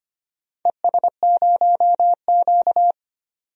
Morse code, heard as a keyed tone, about 25 wpm, 700 Hz. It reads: ES0Q
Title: DXCC-100 Call Signs - 4 Characters - 25wpm